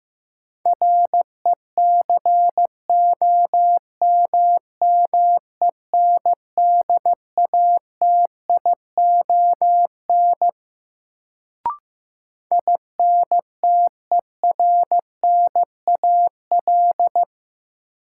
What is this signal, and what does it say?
Morse code, heard as a keyed tone, about 15 wpm, 700 Hz. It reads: RECOMMENDATION INTERNAL
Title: Sentences from Top 1400 Words - New Word Drill - 15wpm